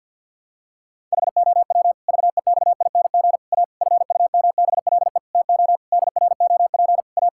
Morse code, heard as a keyed tone, about 50 wpm, 700 Hz. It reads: HOW VEXING A FUMBLE TO DROP A
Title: Pangram Challenge v2 - 50wpm